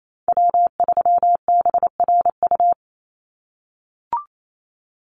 Morse code, heard as a keyed tone, about 28 wpm, 700 Hz. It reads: W3BRU
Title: USA Call Signs 28wpm